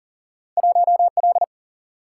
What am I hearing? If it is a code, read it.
Morse code, heard as a keyed tone, about 40 wpm, 700 Hz. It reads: 1L
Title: Single-Letter Single-Number 40wpm